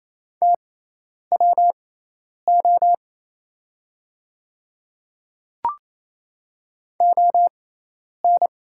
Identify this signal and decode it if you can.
Morse code, heard as a keyed tone, about 28 wpm, spaced as for 10 wpm, 690 Hz. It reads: TWO ON
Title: 6x Character Spacing - Top 100 Words - 28wpm